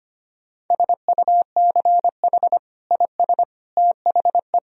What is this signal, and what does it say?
Morse code, heard as a keyed tone, about 25 wpm, 700 Hz. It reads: SUCH IS THE
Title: Sentences from Top 1000 Words - Review Part 3 - 25wpm